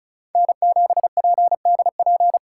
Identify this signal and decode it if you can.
Morse code, heard as a keyed tone, about 36 wpm, 700 Hz. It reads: N7PDP
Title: USA Call Signs - 5 Characters - 3x Character Spacing - 35wpm